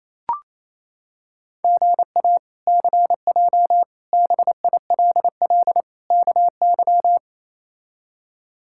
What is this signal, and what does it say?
Morse code, heard as a keyed tone, about 28 wpm, 700 Hz. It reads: GA CJ BILL KY
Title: K1USN SST QSO Practice - 28wpm